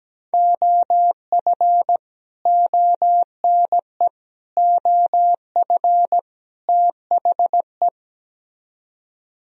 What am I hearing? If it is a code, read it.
Morse code, heard as a keyed tone, about 17 wpm, 700 Hz. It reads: OF ONE OF THE
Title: Speed-Racing - Sets of 4 Words 17wpm